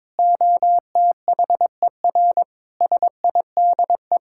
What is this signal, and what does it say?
Morse code, heard as a keyed tone, about 22 wpm, 700 Hz. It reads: OTHER SIDE
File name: practice / 22wpm / Build A Phrase - 5-Word Phrases - 22wpm